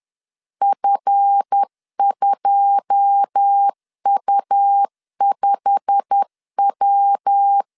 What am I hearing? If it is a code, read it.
Morse code, heard as a keyed tone, about 10 wpm, 800 Hz. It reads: F2U5W
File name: CW / DET / 1 9 1500C S A 10wpm 4511